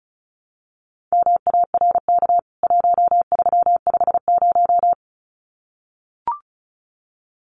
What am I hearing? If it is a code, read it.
Morse code, heard as a keyed tone, about 34 wpm, 700 Hz. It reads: MARK 1350
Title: CWT Exchange 35wpm